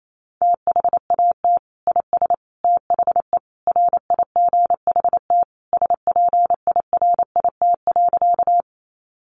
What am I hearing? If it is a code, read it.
Morse code, heard as a keyed tone, about 28 wpm, 700 Hz. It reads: THAT IS THE RIGHT SPIRIT.
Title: Sentences from Top 1400 Words 28wpm